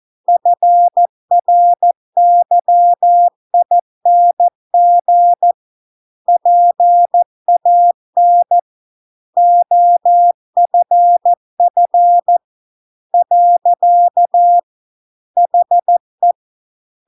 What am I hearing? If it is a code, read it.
Morse code, heard as a keyed tone, about 14 wpm, 700 Hz. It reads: FRYING PAN OFF . HE